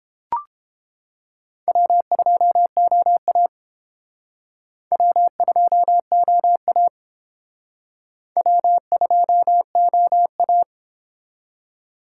Morse code, heard as a keyed tone, about 28 wpm, 700 Hz. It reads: W2OA W2OA W2OA